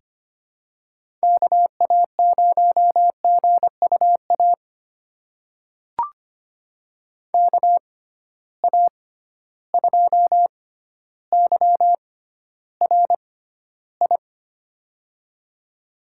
Morse code, heard as a keyed tone, about 25 wpm, spaced as for 13 wpm, 700 Hz. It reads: KA0GUA KA2YRI
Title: USA Call Signs - 6 Characters - 6x Character Spacing - 25wpm